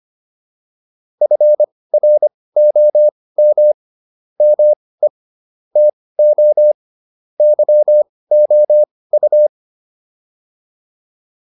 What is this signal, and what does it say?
Morse code, heard as a keyed tone, about 25 wpm, spaced as for 18 wpm, 600 Hz. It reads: FROM ME TO YOU